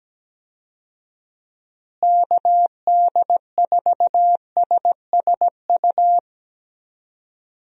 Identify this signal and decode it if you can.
Morse code, heard as a keyed tone, about 17 wpm, 700 Hz. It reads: KD4SSU
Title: USA Call Signs - 6 Characters - Repeated 3x - 17wpm